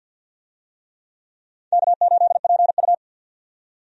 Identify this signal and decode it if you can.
Morse code, heard as a keyed tone, about 49 wpm, 700 Hz. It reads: K8PU
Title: USA Call Signs - 4 Characters - 50wpm